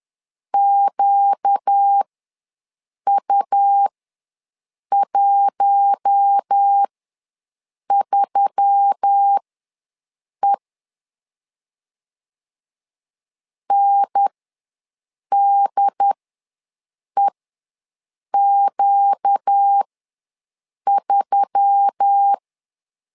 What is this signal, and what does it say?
Morse code, heard as a keyed tone, about 10 wpm, spaced as for 6 wpm, 800 Hz. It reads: QU13E NDEQ3
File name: LEZIONI / 4 Q1S3 ws 10wpm 2437